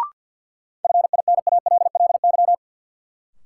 Morse code, heard as a keyed tone, about 46 wpm, 700 Hz. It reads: FINALLY
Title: Sentences from Top 1000 Words - New Word Drill - 50wpm